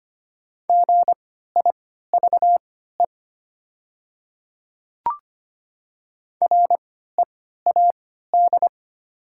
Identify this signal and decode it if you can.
Morse code, heard as a keyed tone, about 25 wpm, spaced as for 14 wpm, 700 Hz. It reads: GIVE READ
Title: 3x Character Spacing - Top 200 Words - 25wpm